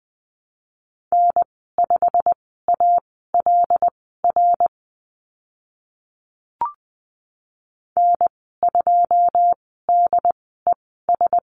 Morse code, heard as a keyed tone, about 20 wpm, spaced as for 15 wpm, 700 Hz. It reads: N5ALR N2DES
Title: 088 Lesson 15g - Number 5 - Call Signs 20wpm